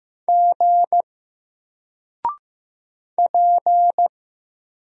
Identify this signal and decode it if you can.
Morse code, heard as a keyed tone, about 15 wpm, 700 Hz. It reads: G P
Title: Single Character (Letter:Number:Punctuation:Prosign) 15wpm